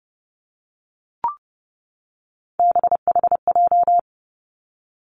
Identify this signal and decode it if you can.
Morse code, heard as a keyed tone, about 30 wpm, 700 Hz. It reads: BHJ
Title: Three Challenging Letter-Number Combinations 30wpm